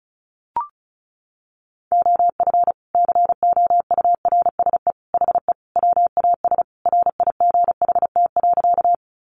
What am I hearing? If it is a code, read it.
Morse code, heard as a keyed tone, about 34 wpm, 700 Hz. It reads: OF COURSE HE WAS RIGHT.
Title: Sentences from Top 500 Words 35wpm